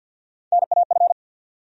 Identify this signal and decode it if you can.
Morse code, heard as a keyed tone, about 46 wpm, 700 Hz. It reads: NAF
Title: USA Call Sign Postfixes - 3 Letters 50wpm